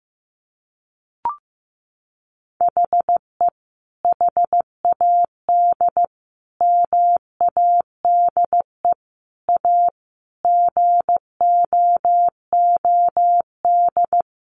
Morse code, heard as a keyed tone, about 15 wpm, 700 Hz. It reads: HE HAD MADE A GOOD
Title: Sentences from Top 900 Words 15wpm